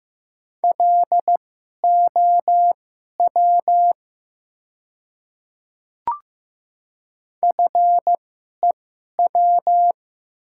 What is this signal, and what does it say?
Morse code, heard as a keyed tone, about 15 wpm, spaced as for 11 wpm, 700 Hz. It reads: LOW FEW
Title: 2x Character Spacing - Top 300 Words - 15wpm